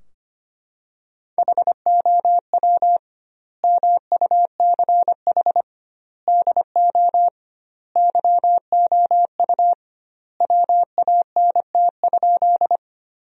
Morse code, heard as a keyed tone, about 25 wpm, 700 Hz. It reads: HOW MUCH DO YOU WANT?
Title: 2x Word-Spacing - Sentences from Top 200 Words - 25wpm